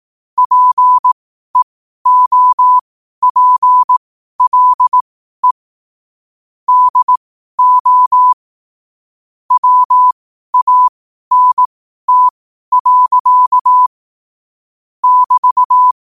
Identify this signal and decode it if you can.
Morse code, heard as a keyed tone, about 18 wpm, spaced as for 12 wpm, 1000 Hz. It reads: PEOPLE DO WANT. <BT>
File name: qotdfiles / 12wpm